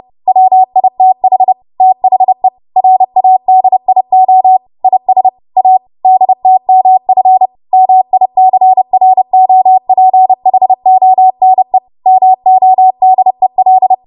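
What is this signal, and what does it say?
Morse code, heard as a keyed tone, about 30 wpm, 750 Hz. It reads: WITH THE RADIO IS A DTMF MICROPHONE MODEL
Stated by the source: Morse practice transmission at 30 WPM.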